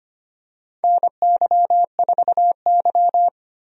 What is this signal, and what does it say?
Morse code, heard as a keyed tone, about 25 wpm, 700 Hz. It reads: NY4Y
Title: USA Call Signs - 4 Characters - Repeated 2x - 25wpm